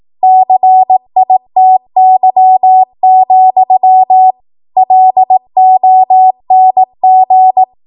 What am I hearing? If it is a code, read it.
Morse code, heard as a keyed tone, about 18 wpm, 750 Hz. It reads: CITY, LONG
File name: Archive / 18 WPM / 18WPM